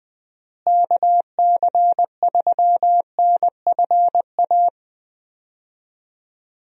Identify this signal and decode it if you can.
Morse code, heard as a keyed tone, about 20 wpm, 700 Hz. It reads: KC3NFA
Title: USA Call Signs - 6 Characters - Repeated 2x - 20wpm